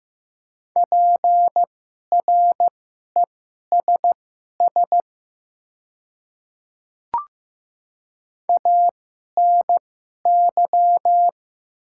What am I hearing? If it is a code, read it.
Morse code, heard as a keyed tone, about 15 wpm, spaced as for 11 wpm, 700 Hz. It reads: PRESS ANY
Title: 2x Character Spacing - Top 500 Words - 15wpm